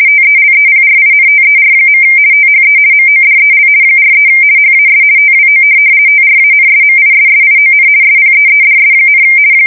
RTTY, about 50 baud. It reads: TELEPRINTER. THE MODEL 43 HAS TWO TRANSMISSION SPEEDS,
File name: fsk170-50B / teletype-corp